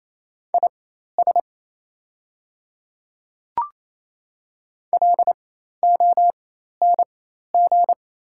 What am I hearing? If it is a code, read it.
Morse code, heard as a keyed tone, about 27 wpm, spaced as for 13 wpm, 700 Hz. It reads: IS LONG
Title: 4x Character Spacing - Top 500 Words - 28wpm